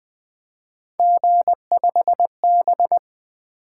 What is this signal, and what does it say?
Morse code, heard as a keyed tone, about 20 wpm, 700 Hz. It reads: G5B
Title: Three Letter-Number Combinations - 20wpm